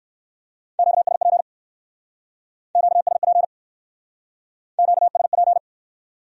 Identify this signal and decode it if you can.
Morse code, heard as a keyed tone, about 57 wpm, 700 Hz. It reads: QSP QSP QSP